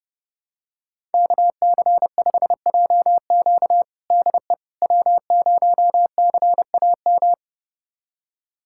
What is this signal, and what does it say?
Morse code, heard as a keyed tone, about 31 wpm, 700 Hz. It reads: KC5JQ DE W0CAM
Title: USA Callsign DE Callsign - 5 Characters - 30wpm